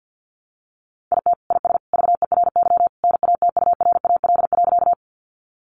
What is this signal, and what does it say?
Morse code, heard as a keyed tone, about 50 wpm, 700 Hz. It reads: IT IS VERY NATURAL.
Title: Sentences from Top 700 Words 50wpm